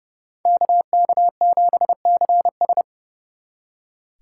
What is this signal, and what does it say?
Morse code, heard as a keyed tone, about 37 wpm, 700 Hz. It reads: KK7CS
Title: Speed Racing - USA Call Signs 30wpm